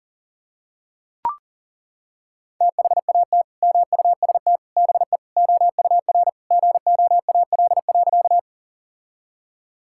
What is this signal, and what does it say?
Morse code, heard as a keyed tone, about 40 wpm, 700 Hz. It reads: THAT MUST BE OUR GOAL.
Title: Sentences from Top 1000 Words - Review Part 5 - 40wpm